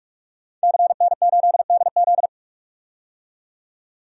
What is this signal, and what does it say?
Morse code, heard as a keyed tone, about 46 wpm, 700 Hz. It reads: CN8DZ
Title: DXCC-100 Call Signs - 5 Characters - 45wpm